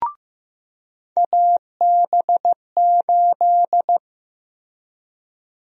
Morse code, heard as a keyed tone, about 15 wpm, 700 Hz. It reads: AB8
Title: USA Call Sign Prefixes - Repeated 2x - 15wpm